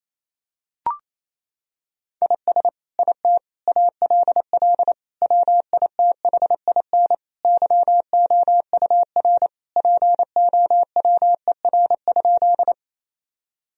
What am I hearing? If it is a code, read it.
Morse code, heard as a keyed tone, about 28 wpm, 700 Hz. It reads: IS IT ALL WITHIN YOUR POWER?
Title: Sentences from Top 1000 Words - Review Part 2 - 28wpm